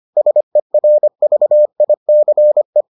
Morse code, heard as a keyed tone, about 25 wpm, 600 Hz. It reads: SERVICE